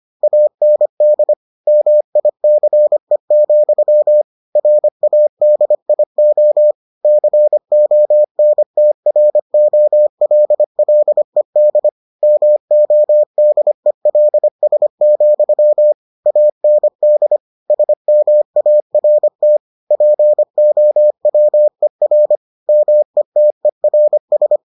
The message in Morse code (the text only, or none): AND MICE, RADIO CONTROLLED MODELS, AND SMART POWER METERS